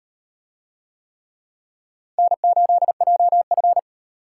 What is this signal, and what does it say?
Morse code, heard as a keyed tone, about 36 wpm, 700 Hz. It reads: N8JF